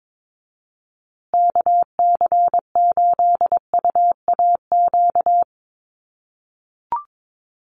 Morse code, heard as a keyed tone, about 22 wpm, 700 Hz. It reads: KC8UAQ
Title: USA Call Signs Repeated 3x 22wpm